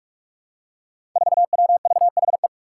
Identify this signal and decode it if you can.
Morse code, heard as a keyed tone, about 43 wpm, 700 Hz. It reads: 4WVHE
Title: Five Challenging Letter-Number Combinations 45wpm